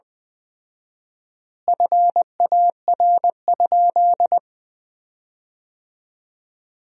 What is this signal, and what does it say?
Morse code, heard as a keyed tone, about 20 wpm, 700 Hz. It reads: FAR?
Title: 4x Word-Spacing - Sentences from Top 300 Words - 20wpm